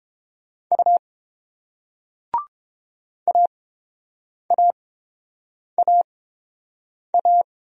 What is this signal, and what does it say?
Morse code, heard as a keyed tone, about 29 wpm, spaced as for 7 wpm, 690 Hz. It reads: U AAAA